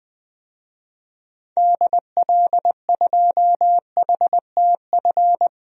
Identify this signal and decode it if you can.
Morse code, heard as a keyed tone, about 20 wpm, 700 Hz. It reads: DL2HTF
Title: DXCC-100 Call Signs - 6 Characters - 20wpm